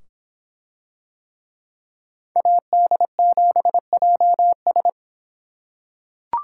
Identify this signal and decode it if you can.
Morse code, heard as a keyed tone, about 23 wpm, 700 Hz. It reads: AD7JS